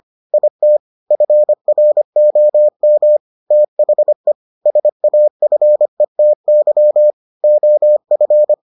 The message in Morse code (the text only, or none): IT FROM THE SAFETY OF